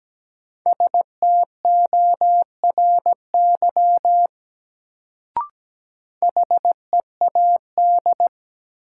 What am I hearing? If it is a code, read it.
Morse code, heard as a keyed tone, about 17 wpm, 700 Hz. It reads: STORY HEAD
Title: Sentences from Top 300 Words - New Word Drill - 17wpm